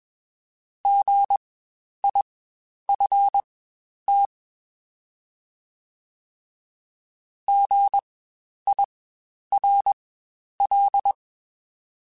Morse code, heard as a keyed tone, about 21 wpm, spaced as for 10 wpm, 800 Hz. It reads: GIFT GIRL
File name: Spaced Alphas